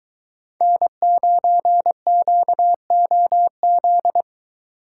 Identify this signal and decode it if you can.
Morse code, heard as a keyed tone, about 23 wpm, 700 Hz. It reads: N9QOZ